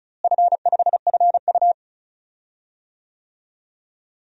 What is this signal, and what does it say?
Morse code, heard as a keyed tone, about 36 wpm, 700 Hz. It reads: F5FU
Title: DXCC-100 Call Signs - 4 Characters - Repeated 2x - 35wpm